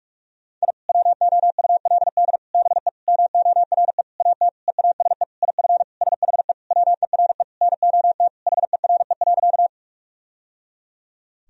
Morse code, heard as a keyed tone, about 46 wpm, 700 Hz. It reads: I WOULD BE MORE AT EASE IF SHE WERE NOT HERE.
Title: Sentences from Top 1000 Words - Review Part 1 - 45wpm